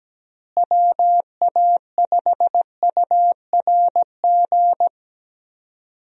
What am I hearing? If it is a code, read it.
Morse code, heard as a keyed tone, about 17 wpm, 700 Hz. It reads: WA5URG
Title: USA Call Signs - 6 Characters - Repeated 2x - 17wpm